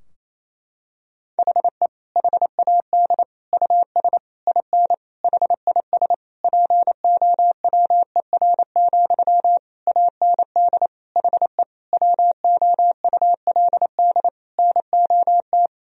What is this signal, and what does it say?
Morse code, heard as a keyed tone, about 27 wpm, 700 Hz. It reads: HE HAD US IN HIS POWER, AND HE WOULD NOT
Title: Sentences from Top 1000 Words - Review Part 2 - 28wpm